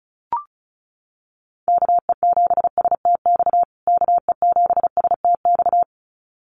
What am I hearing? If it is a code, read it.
Morse code, heard as a keyed tone, about 34 wpm, 700 Hz. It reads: KE7STX KE7STX
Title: USA Call Signs Repeated 2x 35wpm